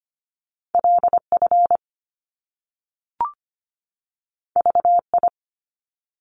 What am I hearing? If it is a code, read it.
Morse code, heard as a keyed tone, about 25 wpm, 700 Hz. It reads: LF VI
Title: Top Two Letter Combinations in Words 25wpm